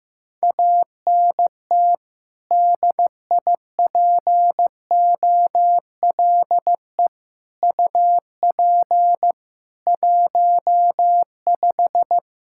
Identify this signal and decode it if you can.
Morse code, heard as a keyed tone, about 15 wpm, 700 Hz. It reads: ANT DIPOLE UP 15
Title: Practice QSO - Exchange 2 - Info - 15wpm